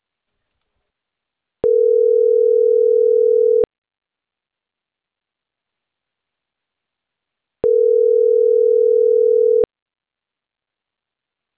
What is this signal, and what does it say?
Ringback tone (North America), 440 + 480 Hz.